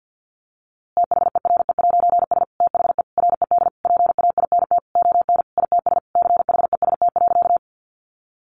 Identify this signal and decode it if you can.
Morse code, heard as a keyed tone, about 50 wpm, 700 Hz. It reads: THERE'S THE RED PAINT ON ITS CHEST.
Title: A Sound Of Thunder 50wpm